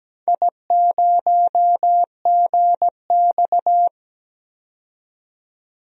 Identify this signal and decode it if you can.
Morse code, heard as a keyed tone, about 17 wpm, 700 Hz. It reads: I0GX
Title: DXCC-100 Call Signs - 4 Characters - 17wpm